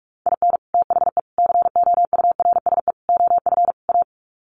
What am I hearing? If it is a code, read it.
Morse code, heard as a keyed tone, about 44 wpm, 700 Hz. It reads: IN THE COURSE OF A
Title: Sets of 5 Words Encore2 45wpm